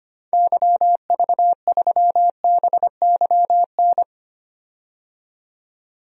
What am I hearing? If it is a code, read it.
Morse code, heard as a keyed tone, about 25 wpm, 700 Hz. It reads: YV3BYN
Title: DXCC-100 Call Signs - 6 Characters - 25wpm